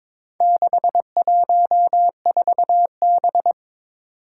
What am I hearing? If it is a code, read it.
Morse code, heard as a keyed tone, about 21 wpm, 700 Hz. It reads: 614B
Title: Four Challenging Letter-Number Combinations 22wpm